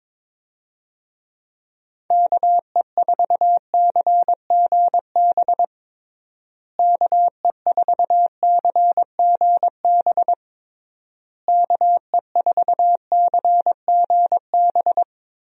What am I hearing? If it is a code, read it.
Morse code, heard as a keyed tone, about 21 wpm, 700 Hz. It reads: KE4CGB KE4CGB KE4CGB
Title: USA Call Signs - 6 Characters - Repeated 3x - 22wpm